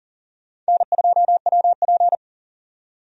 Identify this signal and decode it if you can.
Morse code, heard as a keyed tone, about 40 wpm, 700 Hz. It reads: N2WP
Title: USA Call Signs - 4 Characters - 40wpm